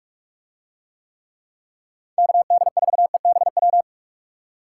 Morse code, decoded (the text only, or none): KD4EBW